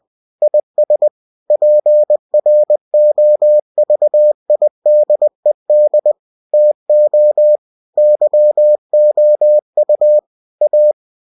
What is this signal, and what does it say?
Morse code, heard as a keyed tone, about 20 wpm, 600 Hz. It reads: IS PROVIDED TO YOU A
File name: cw-books / winnie-the-pooh / winnie-20wpm-page-0119